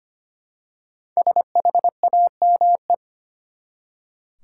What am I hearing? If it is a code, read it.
Morse code, heard as a keyed tone, about 25 wpm, 700 Hz. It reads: SHAME
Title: Five Letter Words - 25wpm